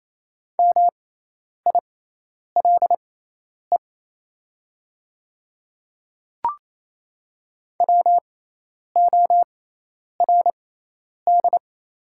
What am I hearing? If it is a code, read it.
Morse code, heard as a keyed tone, about 28 wpm, spaced as for 10 wpm, 700 Hz. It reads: MILE WORD
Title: 6x Character Spacing - Top 500 Words - 28wpm